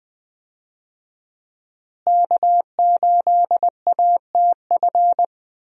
Morse code, heard as a keyed tone, about 20 wpm, 700 Hz. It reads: K8ATF
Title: USA Call Signs - 5 Characters - Repeated 2x - 20wpm